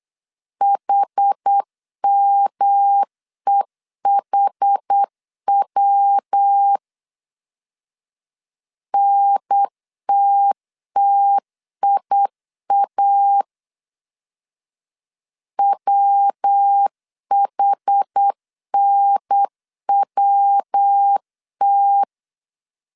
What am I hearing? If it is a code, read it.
Morse code, heard as a keyed tone, about 8 wpm, 800 Hz. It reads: HMEHW NTTIA WHNWT